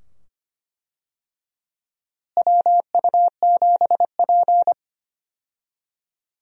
Morse code, heard as a keyed tone, about 25 wpm, 700 Hz. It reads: WU7P
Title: USA Call Signs - 4 Characters - Repeated 2x - 25wpm